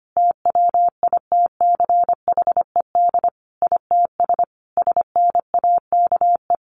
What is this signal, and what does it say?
Morse code, heard as a keyed tone, about 25 wpm, 700 Hz. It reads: TWITCHED ITS SNAKE
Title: A Sound Of Thunder 25wpm